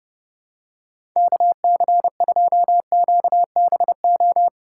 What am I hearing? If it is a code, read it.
Morse code, heard as a keyed tone, about 30 wpm, 700 Hz. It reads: KC2QBO
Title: USA Call Signs - 3x Character Spacing - 30wpm